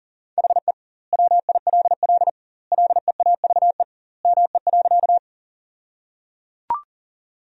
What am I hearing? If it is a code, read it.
Morse code, heard as a keyed tone, about 37 wpm, 700 Hz. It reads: HE WILL LEAVE ME.
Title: 2x Word-Spacing - Sentences from Top 400 Words - 40wpm